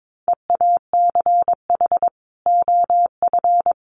Morse code, heard as a keyed tone, about 22 wpm, 700 Hz. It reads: EACH OF
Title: Sentences from Top 1300 Words 22wpm